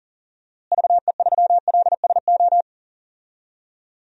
Morse code, heard as a keyed tone, about 40 wpm, 700 Hz. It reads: VE3LSO
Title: DXCC-100 Call Signs - 6 Characters - Repeated 2x - 40wpm